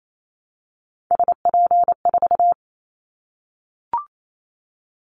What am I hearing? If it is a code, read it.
Morse code, heard as a keyed tone, about 28 wpm, 700 Hz. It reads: SP4
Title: Callsign Prefixes for Top 25 DXCC - Repeated 3x - 28wpm